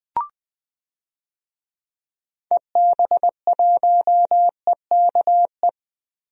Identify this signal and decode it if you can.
Morse code, heard as a keyed tone, about 20 wpm, 700 Hz. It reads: EB1EKE
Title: DXCC-100 Call Signs - 6 Characters - Repeated 2x - 20wpm